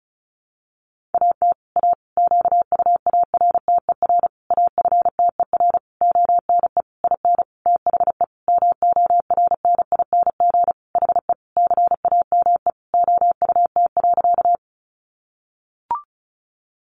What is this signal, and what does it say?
Morse code, heard as a keyed tone, about 34 wpm, 700 Hz. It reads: AT A QUARTER AFTER ONE IN THE MORNING HE CAME OUT.
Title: Sentences from Top 1300 Words 35wpm